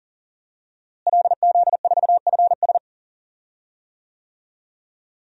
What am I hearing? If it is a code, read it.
Morse code, heard as a keyed tone, about 40 wpm, 700 Hz. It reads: LZ4FS
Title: DXCC-100 Call Signs - 5 Characters - Repeated 3x - 40wpm